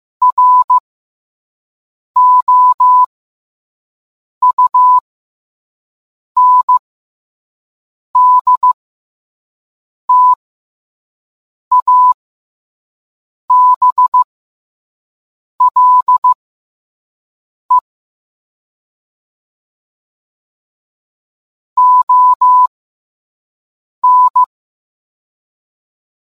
Morse code, heard as a keyed tone, about 15 wpm, spaced as for 5 wpm, 1000 Hz. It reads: ROUNDTABLE ON